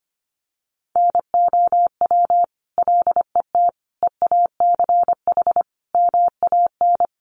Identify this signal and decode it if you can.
Morse code, heard as a keyed tone, about 25 wpm, 700 Hz. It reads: NOW LET EACH MAN
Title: Sentences from Top 1500 Words 25wpm